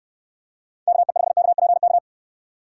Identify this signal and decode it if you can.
Morse code, heard as a keyed tone, about 69 wpm, 700 Hz. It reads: KE5KPK